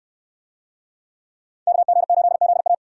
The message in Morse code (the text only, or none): KG8ZA